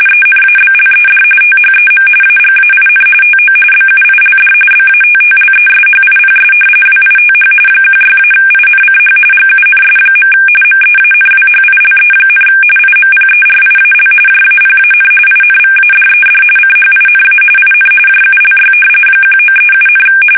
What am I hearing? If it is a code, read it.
RTTY, about 45.45 baud. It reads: ENABLED THEM AGAIN TO BUILD BATTLESHIPS WITH 16-INCH GUNS. MEANWHILE, IMPERIAL JAPAN INVADED CHINA IN JULY 1937,
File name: fsk850 / battleshipnj